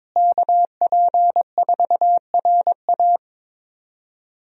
Morse code, heard as a keyed tone, about 22 wpm, 700 Hz. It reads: KP4RA
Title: DXCC-100 Call Signs - 5 Characters - Repeated 3x - 22wpm